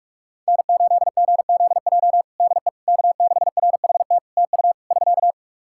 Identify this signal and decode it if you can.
Morse code, heard as a keyed tone, about 48 wpm, 700 Hz. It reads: N8GZJ DE K6RHT TU <SK>
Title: Practice QSO - Exchange 4 - Ending - 45wpm